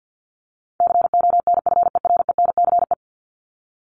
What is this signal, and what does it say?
Morse code, heard as a keyed tone, about 50 wpm, 700 Hz. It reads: CONFERENCE
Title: Top 1751-2000 Words 50wpm